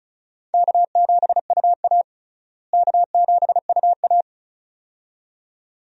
Morse code, heard as a keyed tone, about 36 wpm, 700 Hz. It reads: K7UA K7UA
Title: USA Call Signs - 4 Characters - Repeated 2x - 35wpm